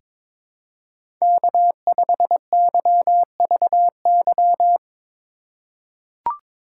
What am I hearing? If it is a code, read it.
Morse code, heard as a keyed tone, about 22 wpm, 700 Hz. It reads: K5YVY
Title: K1USN SST QSO Practice - 22wpm